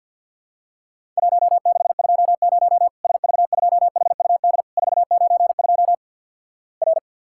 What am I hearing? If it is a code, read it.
Morse code, heard as a keyed tone, about 50 wpm, 700 Hz. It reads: 1620 SV2HUD 492 R
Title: Worked All Europe DX Contest - QTC Exchange - 50wpm